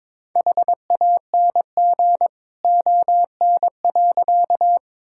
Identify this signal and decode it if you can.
Morse code, heard as a keyed tone, about 22 wpm, 700 Hz. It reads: HANG ON.
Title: Sentences from Top 1600 Words 22wpm